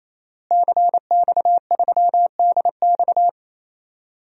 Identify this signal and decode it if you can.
Morse code, heard as a keyed tone, about 27 wpm, 700 Hz. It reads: CX3DX
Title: DXCC-100 Call Signs - 5 Characters - Repeated 3x - 28wpm